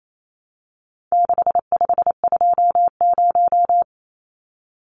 Morse code, heard as a keyed tone, about 28 wpm, 700 Hz. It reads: 6520
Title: Numbers - 4-Digit 28wpm